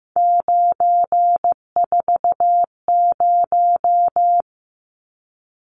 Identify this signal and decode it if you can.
Morse code, heard as a keyed tone, about 15 wpm, 700 Hz. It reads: 940
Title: Numbers - 3-Digit 15wpm